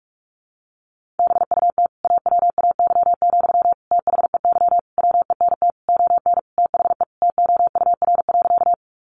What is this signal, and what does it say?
Morse code, heard as a keyed tone, about 44 wpm, 700 Hz. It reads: BUT AWAY, THEY WENT ON THE TOUR.
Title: Sentences from Top 1400 Words 45wpm